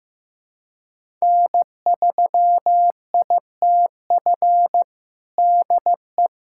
Morse code, heard as a keyed tone, about 16 wpm, 700 Hz. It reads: N3ITF DE
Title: USA Callsign DE Callsign - 5 Characters - 15wpm